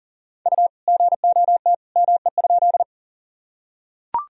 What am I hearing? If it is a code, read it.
Morse code, heard as a keyed tone, about 40 wpm, 700 Hz. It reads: U GOT ME?
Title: Sentences from Top 300 Words and Abbreviations 40wpm